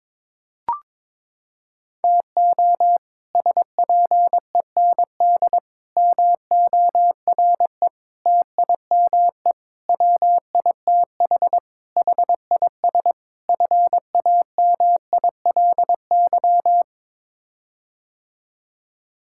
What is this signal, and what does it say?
Morse code, heard as a keyed tone, about 22 wpm, 700 Hz. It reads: TO SPEND MORE TIME WITH HIS FAMILY
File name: practice / 22wpm / Sets of 7 Words - 22wpm